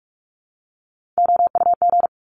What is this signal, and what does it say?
Morse code, heard as a keyed tone, about 46 wpm, 700 Hz. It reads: OUG
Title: Top Three Letter Combinations in Words 45wpm